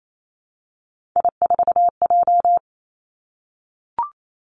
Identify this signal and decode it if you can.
Morse code, heard as a keyed tone, about 28 wpm, 700 Hz. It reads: I4J
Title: Three Challenging Letter-Number Combinations 28wpm